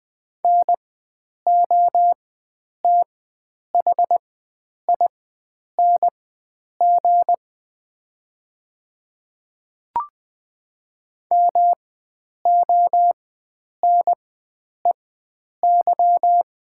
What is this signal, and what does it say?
Morse code, heard as a keyed tone, about 20 wpm, spaced as for 9 wpm, 690 Hz. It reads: NOTHING MONEY
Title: 4x Character Spacing - Top 500 Words - 20wpm